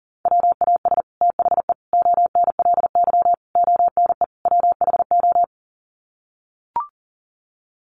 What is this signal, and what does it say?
Morse code, heard as a keyed tone, about 40 wpm, 700 Hz. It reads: WAS THE ONLY ONE WHO
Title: Sets of 5 Words Encore2 40wpm